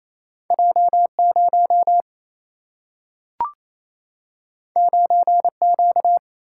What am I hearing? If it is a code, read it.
Morse code, heard as a keyed tone, about 28 wpm, 700 Hz. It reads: J 0 9 Q
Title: Single-Letter Single-Number 28wpm